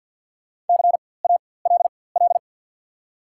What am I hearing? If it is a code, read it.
Morse code, heard as a keyed tone, about 54 wpm, spaced as for 24 wpm, 690 Hz. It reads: CALL
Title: 4x Character Spacing - Top 100 Words - 50wpm